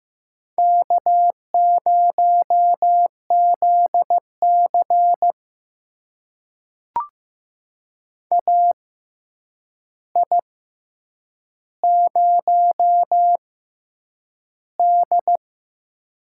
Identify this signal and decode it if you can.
Morse code, heard as a keyed tone, about 15 wpm, spaced as for 8 wpm, 700 Hz. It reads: K0ZC AI0D
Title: USA Call Signs - 4 Characters - 6x Character Spacing - 15wpm